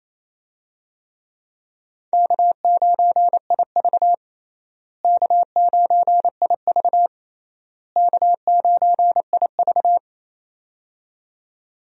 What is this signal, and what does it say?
Morse code, heard as a keyed tone, about 28 wpm, 700 Hz. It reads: K9IV K9IV K9IV
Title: USA Call Signs - 4 Characters - Repeated 3x - 28wpm